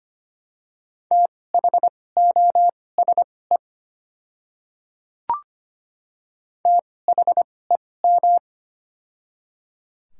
Morse code, heard as a keyed tone, about 25 wpm, spaced as for 18 wpm, 700 Hz. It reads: THOSE THEM
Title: 2x Character Spacing - Top 300 Words - 25wpm